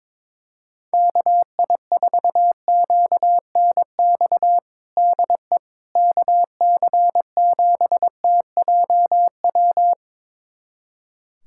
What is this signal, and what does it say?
Morse code, heard as a keyed tone, about 22 wpm, 700 Hz. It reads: KI4QNX DE KC7TJW
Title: USA Callsign DE Callsign - 6 Characters - 22wpm